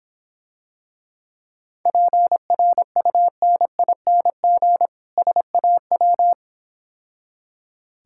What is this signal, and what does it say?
Morse code, heard as a keyed tone, about 26 wpm, 700 Hz. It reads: PRUNING SAW